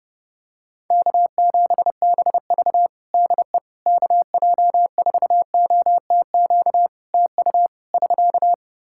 Morse code, heard as a keyed tone, about 30 wpm, 700 Hz. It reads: K7BV DE KJ4OTQ TU <SK>
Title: Practice QSO - Exchange 4 - Ending - 30wpm